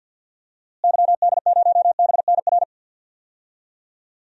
Morse code, heard as a keyed tone, about 50 wpm, 700 Hz. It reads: YD0BNR
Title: DXCC-100 Call Signs - 6 Characters - Repeated 3x - 50wpm